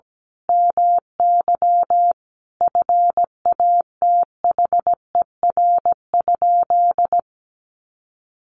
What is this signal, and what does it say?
Morse code, heard as a keyed tone, about 17 wpm, 700 Hz. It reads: MY FATHER?
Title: Sentences from Top 1400 Words 17wpm